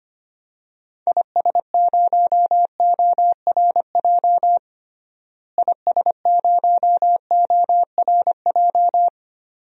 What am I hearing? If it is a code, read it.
Morse code, heard as a keyed tone, about 25 wpm, 700 Hz. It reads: IS0ORJ IS0ORJ
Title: DXCC-100 Call Signs - 6 Characters - Repeated 2x - 25wpm